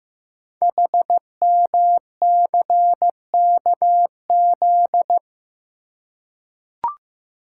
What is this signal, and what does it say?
Morse code, heard as a keyed tone, about 15 wpm, 700 Hz. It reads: HMCKZ
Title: Code Groups - 5 Letters - 15wpm